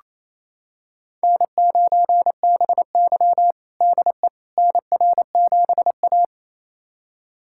Morse code, heard as a keyed tone, about 28 wpm, 700 Hz. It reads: N9BY DE NR7A
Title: USA Callsign DE Callsign - 4 Characters - 28wpm